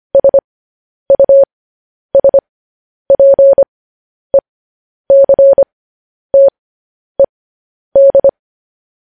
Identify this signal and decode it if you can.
Morse code, heard as a keyed tone, about 25 wpm, spaced as for 10 wpm, 550 Hz. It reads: SUSPECTED
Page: news-headlines-in-morse-code-at-10-wpm